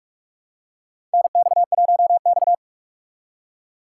Morse code, heard as a keyed tone, about 42 wpm, 700 Hz. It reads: NX1X
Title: USA Call Signs - 4 Characters - 6x Character Spacing - 45wpm